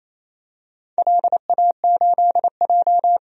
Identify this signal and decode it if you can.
Morse code, heard as a keyed tone, about 27 wpm, 700 Hz. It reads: LA8J
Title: DXCC-100 Call Signs - 4 Characters - 28wpm